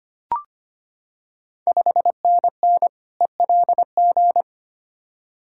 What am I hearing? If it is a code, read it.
Morse code, heard as a keyed tone, about 25 wpm, 700 Hz. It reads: 5NN ELG
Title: Ontario QSO Party Exchange 25wpm